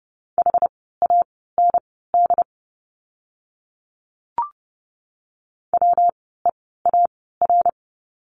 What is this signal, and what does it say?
Morse code, heard as a keyed tone, about 30 wpm, spaced as for 17 wpm, 700 Hz. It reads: HAND WEAR
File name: course / 30:10wpm / 115 Lesson 19f - Letter w - Words 30wpm